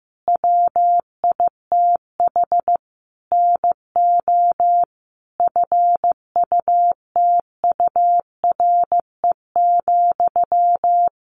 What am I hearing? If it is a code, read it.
Morse code, heard as a keyed tone, about 15 wpm, 700 Hz. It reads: WITH NO FUTURE,
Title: A Sound Of Thunder 15wpm